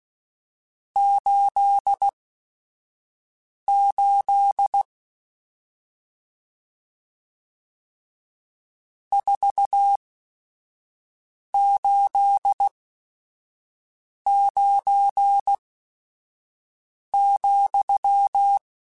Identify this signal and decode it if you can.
Morse code, heard as a keyed tone, about 16 wpm, spaced as for 5 wpm, 800 Hz. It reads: 88 489,
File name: RSTs / RST-mp3s / Various RST @5